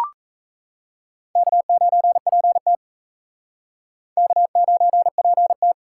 Morse code, heard as a keyed tone, about 34 wpm, 700 Hz. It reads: K9PT K9PT
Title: Speed Racing - USA 1x2 Call Signs 28wpm